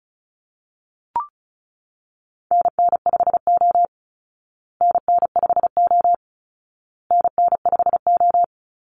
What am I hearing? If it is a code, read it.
Morse code, heard as a keyed tone, about 34 wpm, 700 Hz. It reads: NN5O NN5O NN5O
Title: Call Signs Repeated 2x - 35wpm